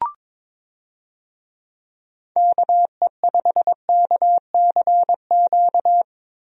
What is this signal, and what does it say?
Morse code, heard as a keyed tone, about 22 wpm, 700 Hz. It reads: KE5KCQ
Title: USA Call Signs - 6 Characters - Repeated 2x - 22wpm